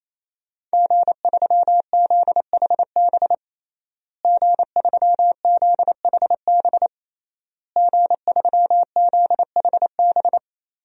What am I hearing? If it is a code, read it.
Morse code, heard as a keyed tone, about 28 wpm, 700 Hz. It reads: G3ZHB G3ZHB G3ZHB
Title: DXCC-100 Call Signs - 5 Characters - Repeated 3x - 28wpm